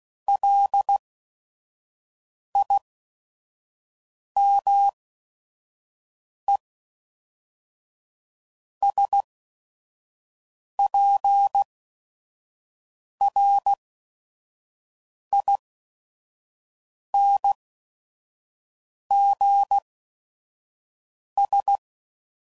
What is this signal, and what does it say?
Morse code, heard as a keyed tone, about 16 wpm, spaced as for 5 wpm, 800 Hz. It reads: LIMESPRINGS
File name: CW Code Groups @5WPM / fromSM's / Cities